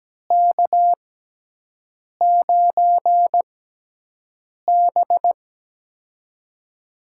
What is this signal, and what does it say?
Morse code, heard as a keyed tone, about 17 wpm, spaced as for 6 wpm, 700 Hz. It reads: K9B
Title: USA Call Signs - 4 Characters - 6x Character Spacing - 17wpm